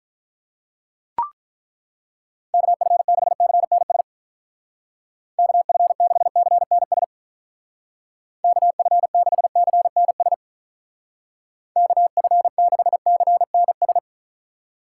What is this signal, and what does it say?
Morse code, heard as a keyed tone, about 43 wpm, 700 Hz. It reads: KF6CNS KF6CNS KF6CNS KF6CNS
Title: Speed Racing - USA 2x3 Call Signs 35wpm